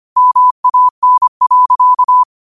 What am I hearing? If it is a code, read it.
Morse code, heard as a keyed tone, about 25 wpm, 1000 Hz. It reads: MAN.
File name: qotdfiles / 25wpm